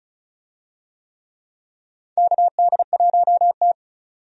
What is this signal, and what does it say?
Morse code, heard as a keyed tone, about 36 wpm, 700 Hz. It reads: KD1T
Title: USA Call Signs - 4 Characters - 35wpm